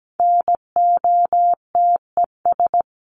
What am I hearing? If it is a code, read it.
Morse code, heard as a keyed tone, about 17 wpm, 700 Hz. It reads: NOTES
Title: ES:S Suffix - Top 100 Words with Base Words 17wpm